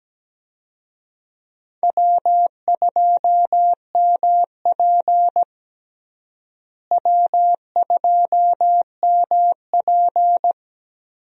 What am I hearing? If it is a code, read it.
Morse code, heard as a keyed tone, about 17 wpm, 700 Hz. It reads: W2MP W2MP
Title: USA Call Signs - 4 Characters - Repeated 3x - 17wpm